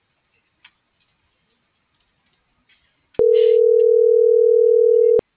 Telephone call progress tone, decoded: ringback tone

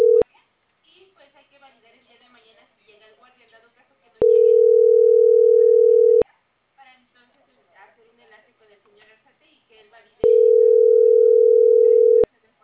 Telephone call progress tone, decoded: ringback tone

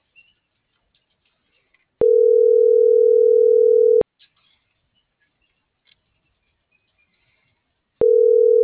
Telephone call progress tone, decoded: ringback tone

